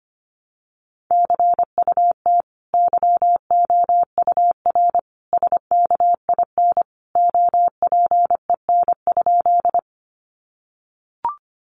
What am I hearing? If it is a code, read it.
Morse code, heard as a keyed tone, about 25 wpm, 700 Hz. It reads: CUT YOUR SKIN OPEN?
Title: Sentences from Top 1300 Words 25wpm